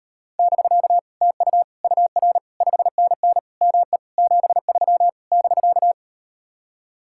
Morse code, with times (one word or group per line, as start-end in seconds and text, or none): 0.4-1.0 <BK>
1.2-1.6 TU
1.8-2.4 UR
2.6-3.4 5NN
3.6-4.0 ME
4.2-5.1 73
5.3-5.9 <BK>